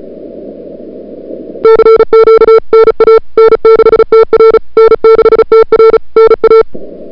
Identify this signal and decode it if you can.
Morse code, heard as a keyed tone, about 34 wpm, 440 Hz. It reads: CQ NA N6TR N6TR NA
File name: CWSprint / February 2007 CW Sprint / 01-2007 CW Sprint Hour1